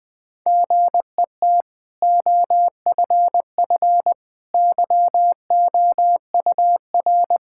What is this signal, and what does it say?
Morse code, heard as a keyed tone, about 20 wpm, 700 Hz. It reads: GET OFF YOUR
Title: Sentences from Top 1600 Words 20wpm